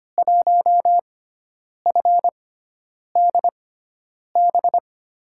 Morse code, heard as a keyed tone, about 25 wpm, spaced as for 9 wpm, 700 Hz. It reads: 1FDB
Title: USA Call Signs - 6 Characters - 6x Character Spacing - 25wpm